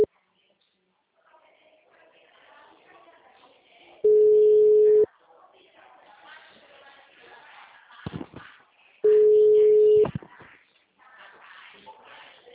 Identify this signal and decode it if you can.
Ringback tone (Europe, 425 Hz), with speech.